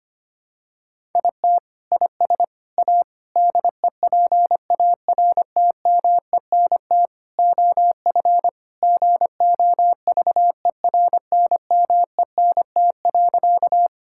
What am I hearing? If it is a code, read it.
Morse code, heard as a keyed tone, about 25 wpm, 700 Hz. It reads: IT IS A DEPARTMENT OF GOVERNMENT.
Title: Sentences from Top 1000 Words - Review Part 5 - 25wpm